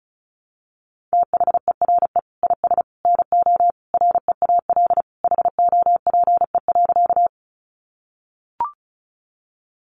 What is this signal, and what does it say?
Morse code, heard as a keyed tone, about 34 wpm, 700 Hz. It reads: THERE IS NO REAL HOPE.
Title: Sentences from Top 1000 Words 35wpm